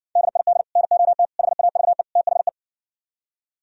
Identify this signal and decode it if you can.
Morse code, heard as a keyed tone, about 69 wpm, 700 Hz. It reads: DID NOT HAVE THE